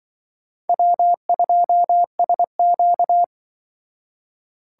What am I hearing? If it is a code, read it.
Morse code, heard as a keyed tone, about 25 wpm, 700 Hz. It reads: W2SQ
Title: Speed Racing - USA 1x2 Call Signs 20wpm